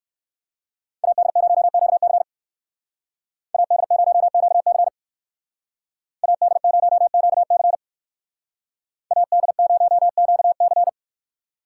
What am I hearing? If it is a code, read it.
Morse code, heard as a keyed tone, about 58 wpm, 700 Hz. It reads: AD0QC AD0QC AD0QC AD0QC